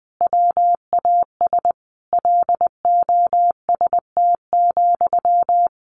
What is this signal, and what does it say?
Morse code, heard as a keyed tone, about 20 wpm, 700 Hz. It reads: WAS LOST,
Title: Sentences from Top 1500 Words 20wpm